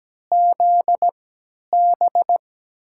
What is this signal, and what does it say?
Morse code, heard as a keyed tone, about 17 wpm, spaced as for 10 wpm, 700 Hz. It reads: ZB
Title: USA Call Signs - 3x Character Spacing - 17wpm